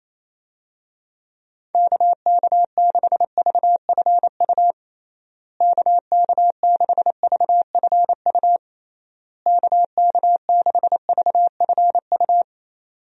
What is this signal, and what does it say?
Morse code, heard as a keyed tone, about 27 wpm, 700 Hz. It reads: KK6VFU KK6VFU KK6VFU
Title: USA Call Signs - 6 Characters - Repeated 3x - 28wpm